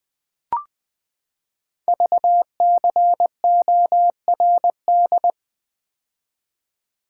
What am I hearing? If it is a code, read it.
Morse code, heard as a keyed tone, about 20 wpm, 700 Hz. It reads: VCORD
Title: Code Groups - 5 Letters - 20wpm